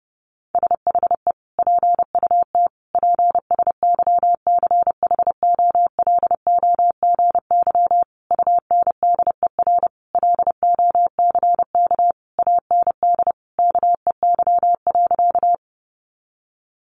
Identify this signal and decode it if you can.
Morse code, heard as a keyed tone, about 30 wpm, 700 Hz. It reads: SHE PUT PSYCHOLOGY UNDER LOCK AND KEY.
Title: Sentences from Top 1400 Words 30wpm